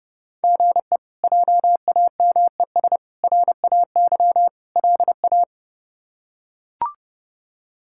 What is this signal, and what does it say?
Morse code, heard as a keyed tone, about 30 wpm, 700 Hz. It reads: GE JAMES RAY LA
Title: K1USN SST QSO Practice - 30wpm